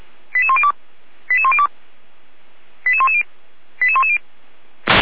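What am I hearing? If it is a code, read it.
Five-tone selective call (ZVEI), heard as quick runs of five tones, tones of about 80 ms. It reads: 88102 88102 88110 88110
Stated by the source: ZVEI five-tone call received on an SDR.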